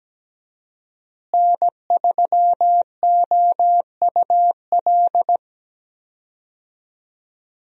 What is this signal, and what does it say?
Morse code, heard as a keyed tone, about 17 wpm, 700 Hz. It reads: N3OUL
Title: USA Call Signs - 5 Characters - Repeated 3x - 17wpm